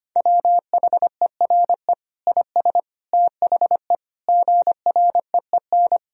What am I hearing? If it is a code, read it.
Morse code, heard as a keyed tone, about 25 wpm, 700 Hz. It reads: WHERE IS THE GREEN
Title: Sentences from Top 1000 Words - Review Part 2 - 25wpm